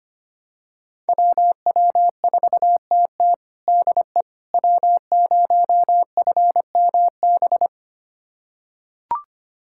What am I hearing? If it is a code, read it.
Morse code, heard as a keyed tone, about 25 wpm, 700 Hz. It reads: WW4TT DE W0FMB
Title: USA Callsign DE Callsign - 5 Characters - 25wpm